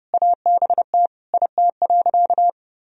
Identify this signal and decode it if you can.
Morse code, heard as a keyed tone, about 32 wpm, 700 Hz. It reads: ABT IT.
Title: Sentences from Top 200 Words and Abbreviations 30wpm